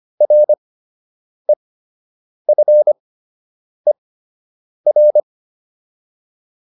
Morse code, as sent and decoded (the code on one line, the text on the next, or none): .-. ..-. .-.
REFER